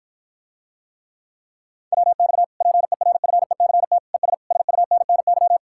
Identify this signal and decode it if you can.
Morse code, heard as a keyed tone, about 54 wpm, 700 Hz. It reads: WX PERFECT ES SUNNY